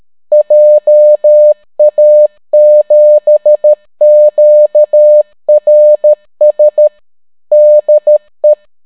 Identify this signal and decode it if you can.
Morse code, heard as a keyed tone, about 13 wpm, 600 Hz. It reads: JA7QRS DE